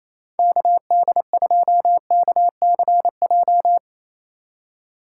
Morse code, heard as a keyed tone, about 28 wpm, 700 Hz. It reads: KD2KCJ
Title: USA Call Signs - 6 Characters - 4x Character Spacing - 28wpm